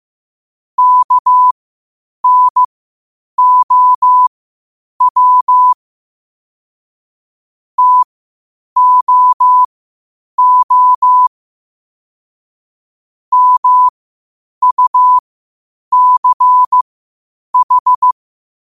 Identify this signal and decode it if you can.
Morse code, heard as a keyed tone, about 15 wpm, spaced as for 8 wpm, 1000 Hz. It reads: KNOW TOO MUCH
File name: qotdfiles / 8wpm